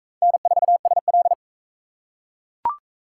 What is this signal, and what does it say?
Morse code, heard as a keyed tone, about 41 wpm, 700 Hz. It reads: N4SL